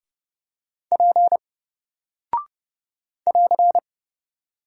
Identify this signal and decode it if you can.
Morse code, heard as a keyed tone, about 30 wpm, 700 Hz. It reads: P <AR>
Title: Single Character (Letter:Number:Punctuation:Prosign) Alt 30wpm